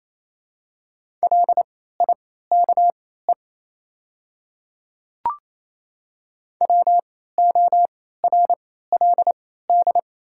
Morse code, heard as a keyed tone, about 28 wpm, spaced as for 16 wpm, 700 Hz. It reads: LIKE WORLD
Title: 3x Character Spacing - Top 300 Words - 28wpm